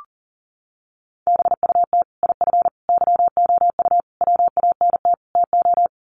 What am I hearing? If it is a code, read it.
Morse code, heard as a keyed tone, about 40 wpm, 700 Hz. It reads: BUT IF YOU WANT TO
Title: Sets of 5 Words Encore2 40wpm